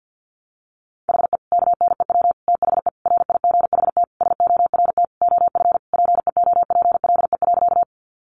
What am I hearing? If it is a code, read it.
Morse code, heard as a keyed tone, about 50 wpm, 700 Hz. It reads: HE KNEW THE RIGHT SORT OF PEOPLE.
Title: Sentences from Top 1300 Words 50wpm